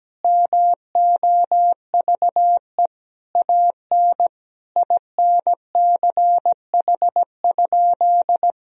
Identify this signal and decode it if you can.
Morse code, heard as a keyed tone, about 17 wpm, 700 Hz. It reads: MOVE AN INCH?
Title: Sentences from Top 1000 Words - Review Part 3 - 17wpm